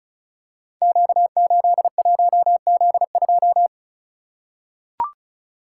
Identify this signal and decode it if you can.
Morse code, heard as a keyed tone, about 36 wpm, 700 Hz. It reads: Q81Z2
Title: Code Groups - 5 Letter:Numbers - 35wpm